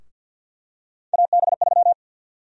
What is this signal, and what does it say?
Morse code, heard as a keyed tone, about 47 wpm, 700 Hz. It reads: AB3
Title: USA Call Sign Prefixes - 50wpm